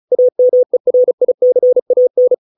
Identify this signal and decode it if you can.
Morse code, heard as a keyed tone, about 34 wpm, 500 Hz. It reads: AMERICAN